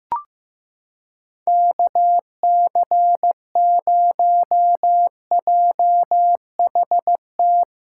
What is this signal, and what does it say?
Morse code, heard as a keyed tone, about 16 wpm, 700 Hz. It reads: KC0JHT
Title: USA Callsign DE Callsign - 6 Characters - 15wpm